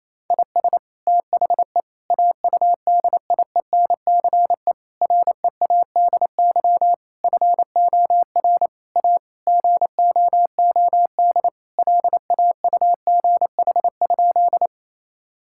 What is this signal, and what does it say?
Morse code, heard as a keyed tone, about 27 wpm, 700 Hz. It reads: IS THE AUDIENCE READY FOR A GOOD LAUGH?
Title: Sentences from Top 1000 Words - Review Part 5 - 28wpm